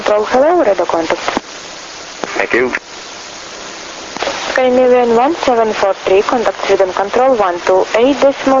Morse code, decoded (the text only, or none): none